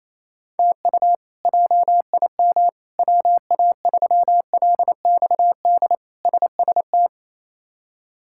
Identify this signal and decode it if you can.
Morse code, heard as a keyed tone, about 28 wpm, 700 Hz. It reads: TU JIM WA3LXD SST
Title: K1USN SST QSO Practice - 28wpm